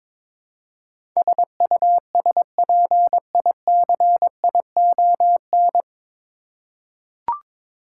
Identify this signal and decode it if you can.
Morse code, heard as a keyed tone, about 21 wpm, 700 Hz. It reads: SUSPICION
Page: Morse Code Ninja practice 22wpm